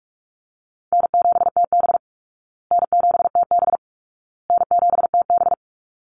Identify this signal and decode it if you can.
Morse code, heard as a keyed tone, about 44 wpm, 700 Hz. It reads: N7TB N7TB N7TB
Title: Easy Call Signs Repeated 3x - 45wpm